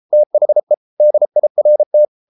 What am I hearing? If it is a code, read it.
Morse code, heard as a keyed tone, about 33 wpm, 600 Hz. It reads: THE DIRT